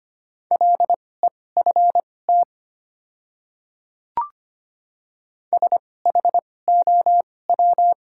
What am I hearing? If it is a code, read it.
Morse code, heard as a keyed tone, about 25 wpm, spaced as for 18 wpm, 700 Hz. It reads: LEFT SHOW
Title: 2x Character Spacing - Top 300 Words - 25wpm